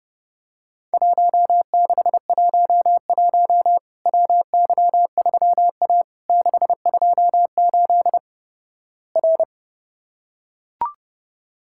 Morse code, heard as a keyed tone, about 32 wpm, 700 Hz. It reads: 1611 WY3A 628 R
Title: Worked All Europe DX Contest - QTC Exchange - 30wpm